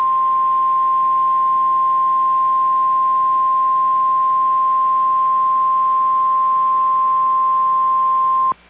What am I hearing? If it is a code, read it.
Emergency Alert System (SAME) tones (the 1050 Hz attention tone). It is heard from 0.5 to 8.0 s.
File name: EAS Logs / KMNR